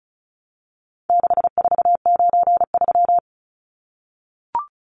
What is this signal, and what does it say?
Morse code, heard as a keyed tone, about 34 wpm, 700 Hz. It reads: 6493
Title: Numbers - 4-Digit 35wpm